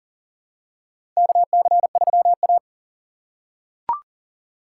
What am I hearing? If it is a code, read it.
Morse code, heard as a keyed tone, about 40 wpm, 700 Hz. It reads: KC3A
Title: USA Call Signs - 4 Characters - 2x Character Spacing - 40wpm